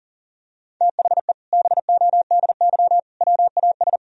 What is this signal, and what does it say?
Morse code, heard as a keyed tone, about 40 wpm, 700 Hz. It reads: THE BODY WAS
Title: Sentences from Top 1000 Words - Review Part 2 - 40wpm